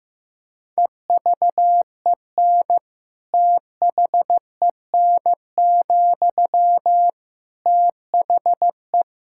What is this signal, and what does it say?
Morse code, heard as a keyed tone, about 15 wpm, 700 Hz. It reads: EVEN THEN, THE
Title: Sentences from Top 1600 Words 15wpm